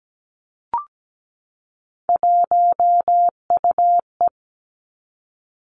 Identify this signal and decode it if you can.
Morse code, heard as a keyed tone, about 17 wpm, 700 Hz. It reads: 1UE
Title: Three Challenging Letter-Number Combinations 17wpm